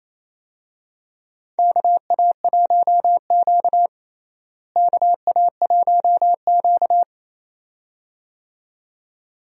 Morse code, heard as a keyed tone, about 27 wpm, 700 Hz. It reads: KA1Q KA1Q
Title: USA Call Signs - 4 Characters - Repeated 2x - 28wpm